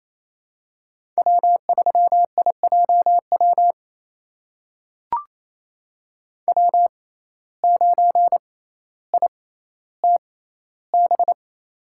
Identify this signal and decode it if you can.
Morse code, heard as a keyed tone, about 28 wpm, spaced as for 14 wpm, 700 Hz. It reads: W3IJW W9ITB
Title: USA Call Signs - 5 Characters - 6x Character Spacing - 28wpm